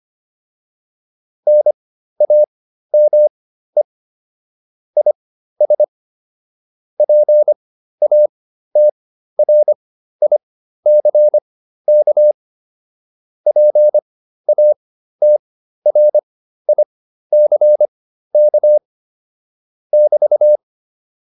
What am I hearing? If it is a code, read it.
Morse code, heard as a keyed tone, about 25 wpm, spaced as for 13 wpm, 600 Hz. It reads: NAME IS PATRICK PATRICK <BT>